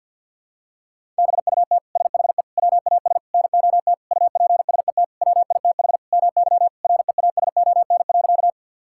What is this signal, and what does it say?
Morse code, heard as a keyed tone, about 47 wpm, 700 Hz. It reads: BUT SHE WAS NOT UPSET WITH MY REASON.
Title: Sentences from Top 1000 Words - Review Part 5 - 50wpm